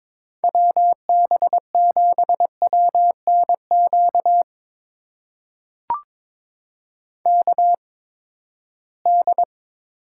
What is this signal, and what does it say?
Morse code, heard as a keyed tone, about 21 wpm, spaced as for 15 wpm, 700 Hz. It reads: WB7WNQ KD
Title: USA Call Signs - 6 Characters - 8x Character Spacing - 22wpm